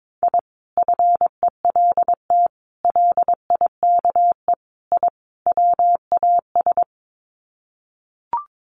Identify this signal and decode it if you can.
Morse code, heard as a keyed tone, about 22 wpm, 700 Hz. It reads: I FELT LIKE I WAS
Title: Sets of 5 Words Encore 22wpm